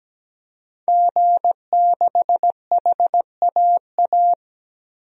Morse code, heard as a keyed tone, about 17 wpm, 700 Hz. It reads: G6HAA
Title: DXCC-100 Call Signs - 5 Characters - 17wpm